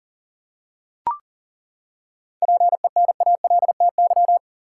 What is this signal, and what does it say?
Morse code, heard as a keyed tone, about 40 wpm, 700 Hz. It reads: PENALTY
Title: Seven Letter Words - 40wpm